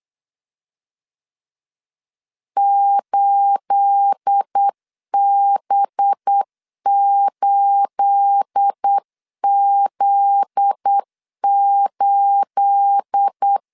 Morse code, heard as a keyed tone, about 8 wpm, 800 Hz. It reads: 8B8Z8
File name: LEZIONI / 9 28 8wpm 3512